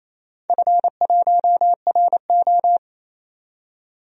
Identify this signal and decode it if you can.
Morse code, heard as a keyed tone, about 28 wpm, 700 Hz. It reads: F1RO
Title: DXCC-100 Call Signs - 4 Characters - 28wpm